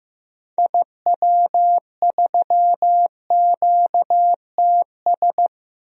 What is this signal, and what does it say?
Morse code, heard as a keyed tone, about 15 wpm, 700 Hz. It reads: IW3QTS
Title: DXCC-100 Call Signs - 6 Characters - 15wpm